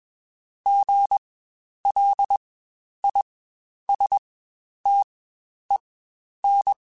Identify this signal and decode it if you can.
Morse code, heard as a keyed tone, about 21 wpm, spaced as for 10 wpm, 800 Hz. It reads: GLISTEN
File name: Spaced Alphas